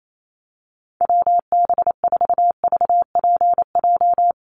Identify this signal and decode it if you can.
Morse code, heard as a keyed tone, about 28 wpm, 700 Hz. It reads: WB4VPJ
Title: USA Call Signs Repeated 2x 28wpm